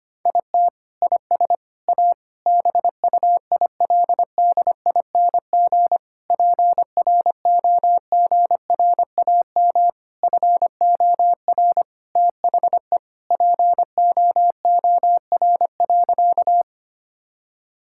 Morse code, heard as a keyed tone, about 25 wpm, 700 Hz. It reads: IT IS A BUILDING PROGRAM FOR THE POOR.
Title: Sentences from Top 1000 Words - Review Part 5 - 25wpm